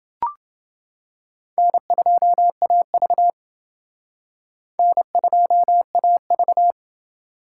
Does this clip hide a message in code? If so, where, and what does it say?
Yes: Morse code from 1.6 to 6.7 s, heard as a keyed tone, about 29 wpm, 700 Hz. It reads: N2AV N2AV